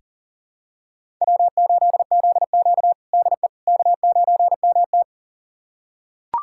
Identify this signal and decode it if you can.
Morse code, heard as a keyed tone, about 37 wpm, 700 Hz. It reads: W8ZQ DE K9MT
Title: USA Callsign DE Callsign - 4 Characters - 40wpm